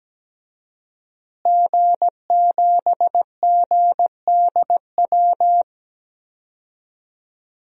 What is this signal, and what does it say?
Morse code, heard as a keyed tone, about 17 wpm, 700 Hz. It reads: G7GDW
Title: DXCC-100 Call Signs - 5 Characters - Repeated 2x - 17wpm